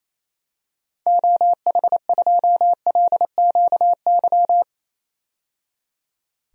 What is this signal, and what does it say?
Morse code, heard as a keyed tone, about 28 wpm, 700 Hz. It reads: OH2LQY
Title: DXCC-100 Call Signs - 6 Characters - 28wpm